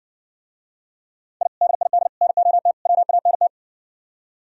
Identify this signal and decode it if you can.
Morse code, heard as a keyed tone, about 62 wpm, 700 Hz. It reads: I DID NOT WANT